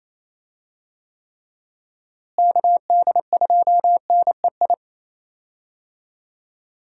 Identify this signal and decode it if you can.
Morse code, heard as a keyed tone, about 28 wpm, 700 Hz. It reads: KD2NEI
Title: USA Call Signs - 6 Characters - Repeated 2x - 28wpm